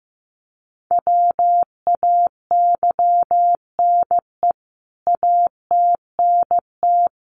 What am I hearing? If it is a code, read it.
Morse code, heard as a keyed tone, about 15 wpm, 700 Hz. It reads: WAYNE ATNT
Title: CWT Exchange 15wpm